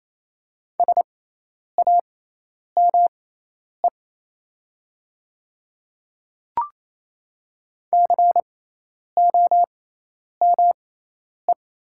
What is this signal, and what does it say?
Morse code, heard as a keyed tone, about 28 wpm, spaced as for 10 wpm, 700 Hz. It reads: SAME COME
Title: 6x Character Spacing - Top 200 Words - 28wpm